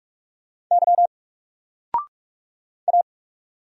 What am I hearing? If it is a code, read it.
Morse code, heard as a keyed tone, about 44 wpm, 700 Hz. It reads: Y A
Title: Single Letter-Number 45wpm